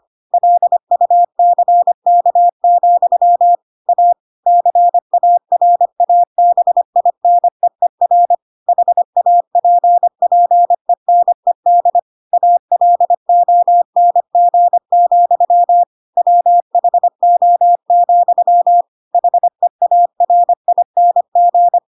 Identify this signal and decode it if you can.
Morse code, heard as a keyed tone, about 25 wpm, 700 Hz. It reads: LUCK, A CARABINEER HAPPENED ALONG, WHO, HEARING
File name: Pinocchio / Pinocchio ch3 25 25 ik1hge